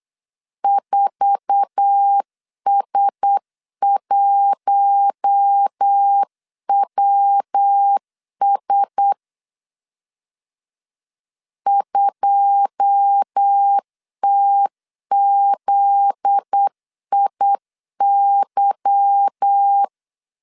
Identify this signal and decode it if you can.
Morse code, heard as a keyed tone, about 8 wpm, 800 Hz. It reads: 4S1WS 2TZIY
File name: CW / DET / 1 9 1500C S A 8wpm 5637